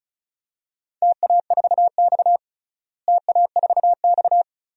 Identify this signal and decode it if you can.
Morse code, heard as a keyed tone, about 36 wpm, 700 Hz. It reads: TA4X TA4X
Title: DXCC-100 Call Signs - 4 Characters - Repeated 3x - 35wpm